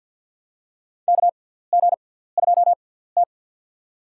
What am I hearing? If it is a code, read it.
Morse code, heard as a keyed tone, about 53 wpm, spaced as for 17 wpm, 700 Hz. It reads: KG2T
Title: USA Call Signs - 4 Characters - 6x Character Spacing - 50wpm